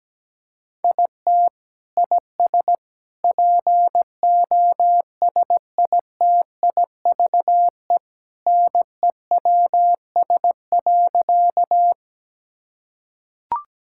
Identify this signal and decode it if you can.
Morse code, heard as a keyed tone, about 17 wpm, 700 Hz. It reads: IT IS POSITIVE NEWS.
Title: Sentences from Top 1000 Words - Review Part 5 - 17wpm